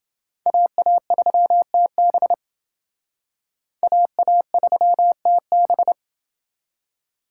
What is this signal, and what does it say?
Morse code, heard as a keyed tone, about 28 wpm, 700 Hz. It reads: AA3TB AA3TB